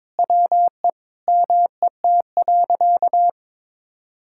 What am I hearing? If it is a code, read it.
Morse code, heard as a keyed tone, about 22 wpm, 700 Hz. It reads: WE MET.
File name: practice / 22wpm / Sentences with 2-Word Phrases - 22wpm